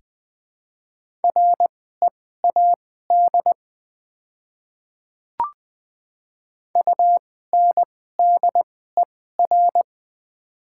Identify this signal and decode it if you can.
Morse code, heard as a keyed tone, about 20 wpm, spaced as for 14 wpm, 700 Hz. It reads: READ UNDER
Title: 2x Character Spacing - Top 200 Words - 20wpm